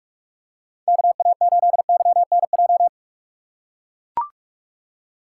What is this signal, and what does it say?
Morse code, heard as a keyed tone, about 44 wpm, 700 Hz. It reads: KA8YNJ
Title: USA Call Signs - 6x Character Spacing - 45wpm